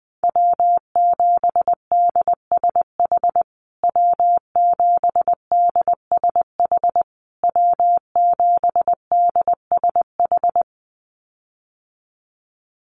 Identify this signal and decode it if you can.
Morse code, heard as a keyed tone, about 20 wpm, 700 Hz. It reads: W7DSH W7DSH W7DSH
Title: USA Call Signs Repeated 3x 20wpm